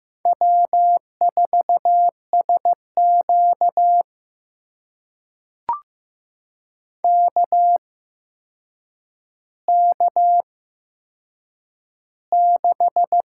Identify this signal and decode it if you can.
Morse code, heard as a keyed tone, about 15 wpm, spaced as for 7 wpm, 700 Hz. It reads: W4SQ KK6
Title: USA Call Signs - 4 Characters - 8x Character Spacing - 15wpm